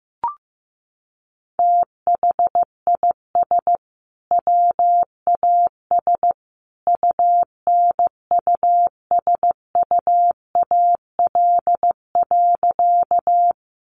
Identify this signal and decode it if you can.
Morse code, heard as a keyed tone, about 15 wpm, 700 Hz. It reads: THIS WAS UNUSUAL.
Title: Sentences from Top 1400 Words 15wpm